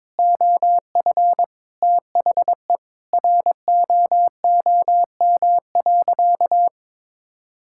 Morse code, heard as a keyed tone, about 22 wpm, 700 Hz. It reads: OF THE ROOM.
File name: practice / 22wpm / Sentences with 2-Word Phrases - 22wpm